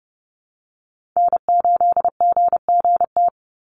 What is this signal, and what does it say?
Morse code, heard as a keyed tone, about 30 wpm, 700 Hz. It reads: N8GGT
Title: USA Call Signs 30wpm